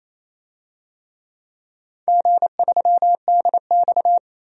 Morse code, heard as a keyed tone, about 27 wpm, 690 Hz. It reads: G3DX
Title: DXCC-100 Call Signs - 4 Characters - 28wpm